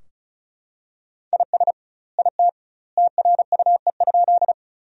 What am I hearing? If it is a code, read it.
Morse code, heard as a keyed tone, about 36 wpm, 700 Hz. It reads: IS IT TRUE?
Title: 2x Word-Spacing - Sentences from Top 400 Words - 35wpm